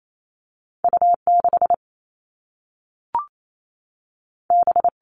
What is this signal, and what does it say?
Morse code, heard as a keyed tone, about 28 wpm, 700 Hz. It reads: U 6 B
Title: All Cut Numbers - 2-Digit 28wpm